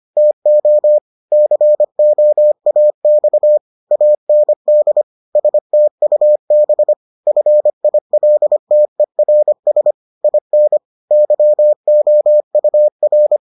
Morse code, heard as a keyed tone, about 25 wpm, 600 Hz. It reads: TO COAX AND STUB FILTERS IN YOUR